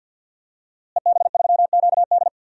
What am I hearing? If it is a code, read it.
Morse code, heard as a keyed tone, about 51 wpm, 700 Hz. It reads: EB3QD
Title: DXCC-100 Call Signs - 5 Characters - Repeated 2x - 50wpm